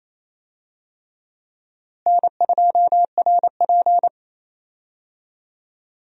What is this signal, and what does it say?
Morse code, heard as a keyed tone, about 28 wpm, 700 Hz. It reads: N2RP
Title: USA Call Signs - 4 Characters - Repeated 3x - 28wpm